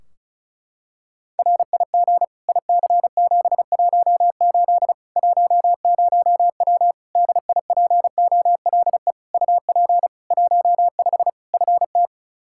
Morse code, heard as a keyed tone, about 36 wpm, 700 Hz. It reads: RIG IC718 10W DIPOLE UP 15 FT
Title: Practice QSO - Exchange 2 - Info - 35wpm